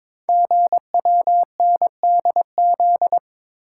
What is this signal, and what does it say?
Morse code, heard as a keyed tone, about 23 wpm, 700 Hz. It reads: GWNDZ
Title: Code Groups - 5 Letters - 22wpm